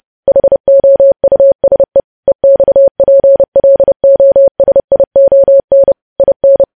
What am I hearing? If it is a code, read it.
Morse code, heard as a keyed tone, about 30 wpm, 550 Hz. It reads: HOUSE EXPLOSION IN
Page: news headlines in morse code at 30 wpm